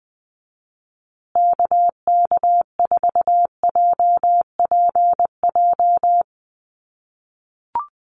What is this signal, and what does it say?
Morse code, heard as a keyed tone, about 20 wpm, 700 Hz. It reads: KK4JPJ
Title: USA Call Signs Repeated 2x 20wpm